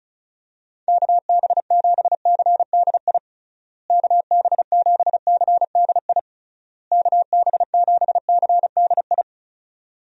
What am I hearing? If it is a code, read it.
Morse code, heard as a keyed tone, about 36 wpm, 700 Hz. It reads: KB7CDI KB7CDI KB7CDI
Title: USA Call Signs - 6 Characters - Repeated 3x - 35wpm